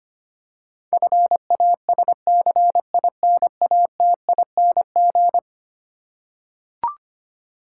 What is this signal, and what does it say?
Morse code, heard as a keyed tone, about 25 wpm, 700 Hz. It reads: FASCINATING
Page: morse-code-ninja-practice-25wpm